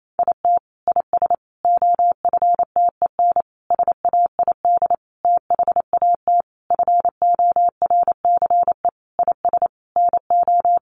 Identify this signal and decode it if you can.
Morse code, heard as a keyed tone, about 28 wpm, 700 Hz. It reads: IT IS OFTEN SAID THAT FORCE IS NO
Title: Sentences from Top 1400 Words 28wpm